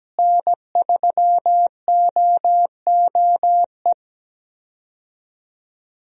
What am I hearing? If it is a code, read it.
Morse code, heard as a keyed tone, about 17 wpm, 700 Hz. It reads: N3OOE
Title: USA Call Signs - 5 Characters - Repeated 2x - 17wpm